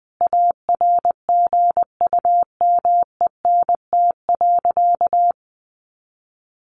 Morse code, heard as a keyed tone, about 20 wpm, 700 Hz. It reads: ARGUMENT.
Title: Sentences from Top 1500 Words 20wpm